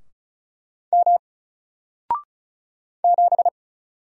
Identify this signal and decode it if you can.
Morse code, heard as a keyed tone, about 35 wpm, 700 Hz. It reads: M 7
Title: Single Letter-Number Alt 35wpm